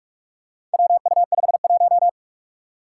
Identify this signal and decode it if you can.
Morse code, heard as a keyed tone, about 46 wpm, 700 Hz. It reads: WU51
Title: Four Challenging Letter-Number Combinations 45wpm